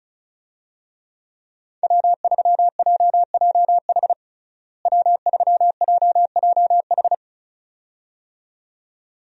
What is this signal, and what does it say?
Morse code, heard as a keyed tone, about 36 wpm, 700 Hz. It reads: W3JJH W3JJH
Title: USA Call Signs - 5 Characters - Repeated 2x - 35wpm